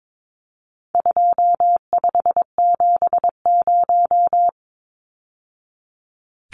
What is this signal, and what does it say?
Morse code, heard as a keyed tone, about 22 wpm, 700 Hz. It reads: 2570
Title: Numbers - 4-Digit 22wpm